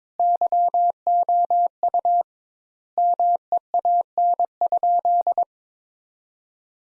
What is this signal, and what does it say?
Morse code, heard as a keyed tone, about 22 wpm, 700 Hz. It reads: YOU MEAN?
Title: Sentences Top 200 Words - 22wpm